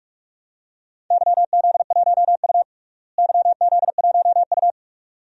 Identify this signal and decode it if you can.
Morse code, heard as a keyed tone, about 46 wpm, 700 Hz. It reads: YZ1U YZ1U
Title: DXCC-100 Call Signs - 4 Characters - Repeated 2x - 45wpm